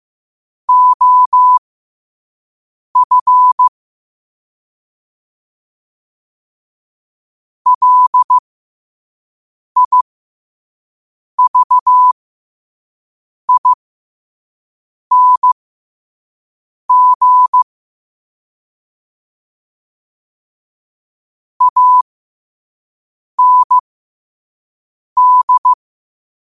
Morse code, heard as a keyed tone, about 15 wpm, spaced as for 5 wpm, 1000 Hz. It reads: OF LIVING AND
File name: qotdfiles / 5wpm